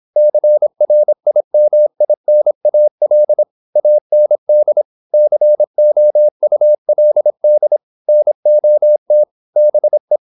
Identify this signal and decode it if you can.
Morse code, heard as a keyed tone, about 26 wpm, 600 Hz. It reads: CRIMINAL AND COULD NOT BE
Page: Never 26wpm | CW med Gnister